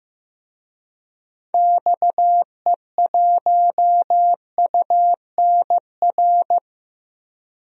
Morse code, heard as a keyed tone, about 15 wpm, 700 Hz. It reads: XE1UNR
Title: DXCC-100 Call Signs - 6 Characters - 15wpm